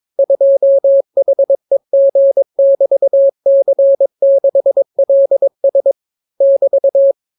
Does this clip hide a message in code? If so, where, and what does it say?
Yes: Morse code from 0.2 to 7.1 s, heard as a keyed tone, about 22 wpm, 550 Hz. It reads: 2HEG<BT>C6LS <BT>